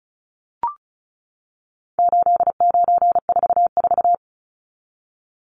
Morse code, heard as a keyed tone, about 34 wpm, 700 Hz. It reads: 8944
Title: Numbers - 4-Digit 35wpm